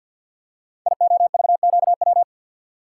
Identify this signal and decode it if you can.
Morse code, heard as a keyed tone, about 53 wpm, 700 Hz. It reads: IOVQW
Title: Code Groups - 5 Letter:Numbers - 50wpm